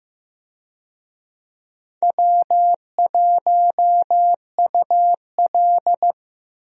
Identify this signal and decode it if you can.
Morse code, heard as a keyed tone, about 15 wpm, 700 Hz. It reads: W1UL
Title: USA Call Signs - 4 Characters - Repeated 2x - 15wpm